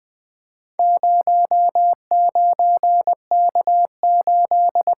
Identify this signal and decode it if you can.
Morse code, heard as a keyed tone, about 20 wpm, 700 Hz. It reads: 09K8
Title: Four Letter-Number Combinations - 20wpm